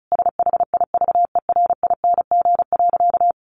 Morse code, heard as a keyed tone, about 34 wpm, 700 Hz. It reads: SHIVERING.
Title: A Sound Of Thunder 35wpm